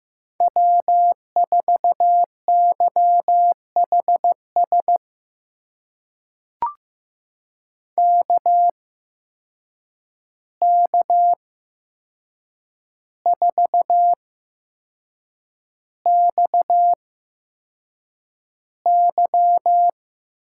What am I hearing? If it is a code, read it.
Morse code, heard as a keyed tone, about 15 wpm, spaced as for 6 wpm, 700 Hz. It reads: W4YHS KK4XY
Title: USA Call Signs - 8x Character Spacing - 15wpm